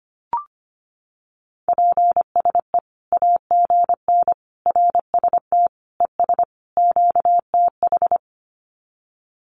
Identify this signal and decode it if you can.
Morse code, heard as a keyed tone, about 25 wpm, 700 Hz. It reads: PSE AGN RST ES QTH
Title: Common QSO Phrases 25wpm